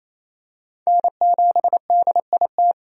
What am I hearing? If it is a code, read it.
Morse code, heard as a keyed tone, about 28 wpm, 700 Hz. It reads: N7DIT
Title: USA Call Signs - 5 Characters - 28wpm